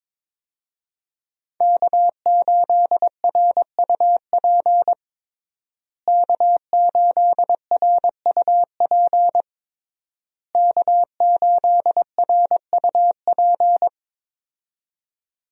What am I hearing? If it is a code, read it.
Morse code, heard as a keyed tone, about 22 wpm, 700 Hz. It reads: K8RUP K8RUP K8RUP
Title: USA Call Signs - 5 Characters - Repeated 3x - 22wpm